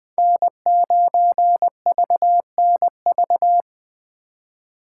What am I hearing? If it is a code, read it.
Morse code, heard as a keyed tone, about 20 wpm, 700 Hz. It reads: N9VNV
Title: Speed Racing - USA 1x3 Call Signs 17wpm